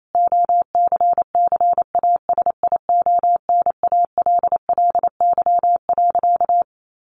Morse code, heard as a keyed tone, about 28 wpm, 700 Hz. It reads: OCCASIONALLY.
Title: Sentences from Top 1300 Words 28wpm